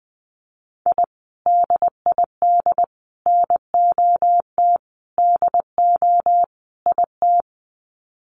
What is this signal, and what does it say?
Morse code, heard as a keyed tone, about 20 wpm, 700 Hz. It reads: I DID NOT DO IT
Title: Sets of 5 Words Encore2 20wpm